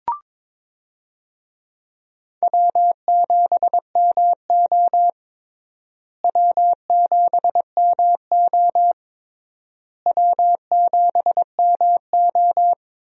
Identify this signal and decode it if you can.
Morse code, heard as a keyed tone, about 22 wpm, 700 Hz. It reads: W7MO W7MO W7MO
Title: USA Call Signs - 4 Characters - Repeated 3x - 22wpm